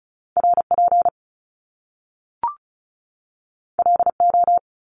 Morse code, heard as a keyed tone, about 36 wpm, 700 Hz. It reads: RP LO
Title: Top Two Letter Combinations in Words 35wpm